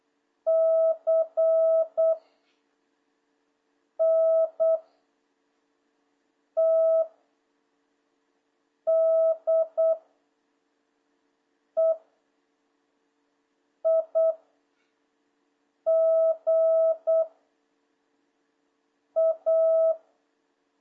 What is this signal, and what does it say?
Morse code, heard as a keyed tone, about 8 wpm, 650 Hz. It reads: C N T D E I G A
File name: jh8cbh / 2026cw